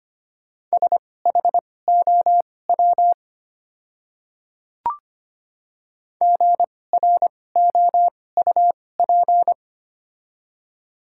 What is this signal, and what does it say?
Morse code, heard as a keyed tone, about 25 wpm, spaced as for 18 wpm, 700 Hz. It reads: SHOW GROUP
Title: 2x Character Spacing - Top 300 Words - 25wpm